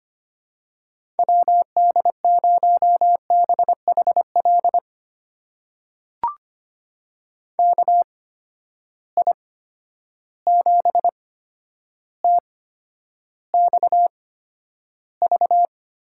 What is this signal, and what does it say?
Morse code, heard as a keyed tone, about 25 wpm, spaced as for 11 wpm, 700 Hz. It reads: WD0BHL KI7TXV
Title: USA Call Signs - 6 Characters - 8x Character Spacing - 25wpm